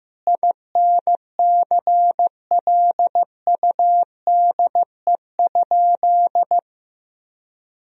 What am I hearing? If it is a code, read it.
Morse code, heard as a keyed tone, about 15 wpm, 700 Hz. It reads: INCLUDE?
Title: Sentences from Top 1000 Words - Review Part 3 - 15wpm